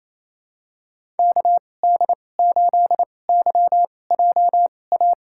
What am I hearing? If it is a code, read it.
Morse code, heard as a keyed tone, about 28 wpm, spaced as for 20 wpm, 700 Hz. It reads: KD8YJA
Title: USA Call Signs - 2x Character Spacing - 28wpm